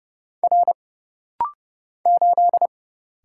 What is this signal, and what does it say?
Morse code, heard as a keyed tone, about 31 wpm, 700 Hz. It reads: R 8
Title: Single Letter-Number - Rapid-Fire 30wpm